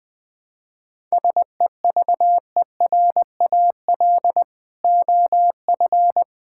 Morse code, heard as a keyed tone, about 21 wpm, 700 Hz. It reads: SEVERAL OF
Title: Sentences from Top 1000 Words - Review Part 5 - 20wpm